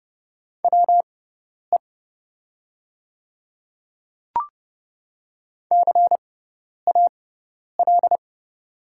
Morse code, heard as a keyed tone, about 30 wpm, spaced as for 10 wpm, 700 Hz. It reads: WE CAL
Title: 6x Character Spacing - Top 200 Words - 30wpm